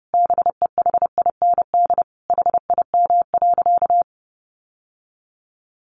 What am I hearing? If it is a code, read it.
Morse code, heard as a keyed tone, about 28 wpm, 700 Hz. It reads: BEHIND HIM.
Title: Sentences from Top 1400 Words 30wpm